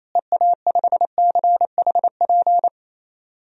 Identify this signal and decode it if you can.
Morse code, heard as a keyed tone, about 28 wpm, 700 Hz. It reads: EA5CHP
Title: DXCC-100 Call Signs - 6 Characters - Repeated 2x - 28wpm